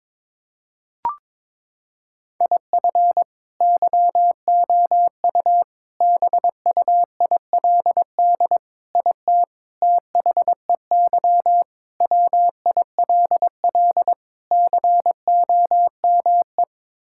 Morse code, heard as a keyed tone, about 22 wpm, 700 Hz. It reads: IF YOU BUILD IT THEY WILL COME
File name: practice / 22wpm / Sets of 7 Words - 22wpm